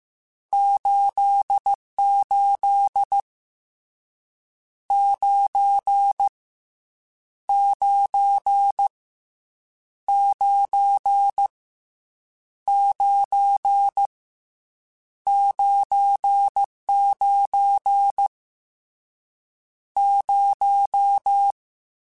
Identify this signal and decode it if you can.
Morse code, heard as a keyed tone, about 15 wpm, 800 Hz. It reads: 88 9 9 9 9 99 0